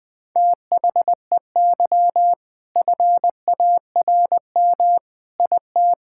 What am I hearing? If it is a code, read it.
Morse code, heard as a keyed tone, about 21 wpm, 700 Hz. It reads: THEY FARM IT
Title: Sentences from Top 1000 Words - Review Part 5 - 20wpm